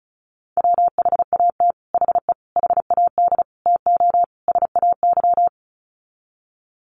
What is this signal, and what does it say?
Morse code, heard as a keyed tone, about 34 wpm, 700 Hz. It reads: WHAT HE HAD TO SAY
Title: Sets of 5 Words Encore2 35wpm